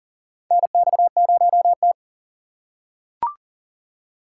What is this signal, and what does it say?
Morse code, heard as a keyed tone, about 40 wpm, 700 Hz. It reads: NX0T
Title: USA Call Signs - 4 Characters - 6x Character Spacing - 40wpm